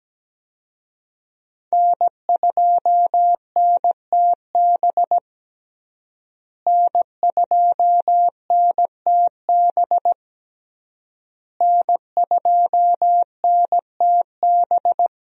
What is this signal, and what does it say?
Morse code, heard as a keyed tone, about 17 wpm, 700 Hz. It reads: N2NTB N2NTB N2NTB
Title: USA Call Signs - 5 Characters - Repeated 3x - 17wpm